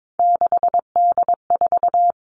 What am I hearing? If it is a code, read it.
Morse code, heard as a keyed tone, about 22 wpm, 700 Hz. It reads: 6D4
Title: Three Challenging Letter-Number Combinations 22wpm